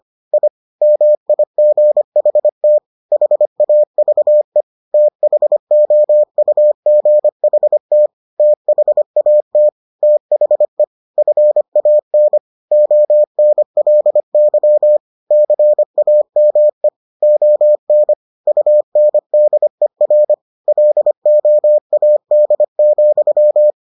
Morse code, heard as a keyed tone, about 25 wpm, 600 Hz. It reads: I MIGHT HAVE THOUGHT THAT THE FAN ONLY CAME ON UNDER LOAD,